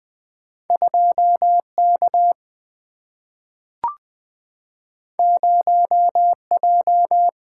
Morse code, heard as a keyed tone, about 20 wpm, 700 Hz. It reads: 2 K 0 J
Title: Single-Letter Single-Number 20wpm